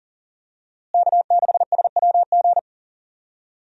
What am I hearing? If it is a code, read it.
Morse code, heard as a keyed tone, about 40 wpm, 700 Hz. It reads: K6SWG
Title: USA Call Signs - 5 Characters - Repeated 3x - 40wpm